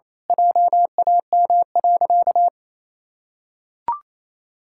Morse code, heard as a keyed tone, about 27 wpm, 700 Hz. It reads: JAM.
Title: Pangram Challenge v2 - 28wpm